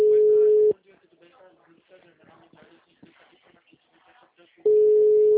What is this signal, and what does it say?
Ringback tone (Europe, 425 Hz), with speech.
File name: out- -2006- - - .35